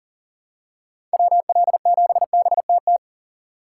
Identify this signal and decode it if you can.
Morse code, heard as a keyed tone, about 40 wpm, 700 Hz. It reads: WL7BTT
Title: USA Call Signs - 6 Characters - 6x Character Spacing - 40wpm